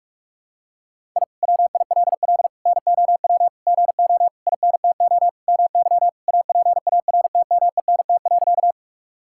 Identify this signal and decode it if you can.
Morse code, heard as a keyed tone, about 46 wpm, 700 Hz. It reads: I WILL NOW GO INTO MY APARTMENT.
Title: Sentences from Top 1000 Words - Review Part 6 - 45wpm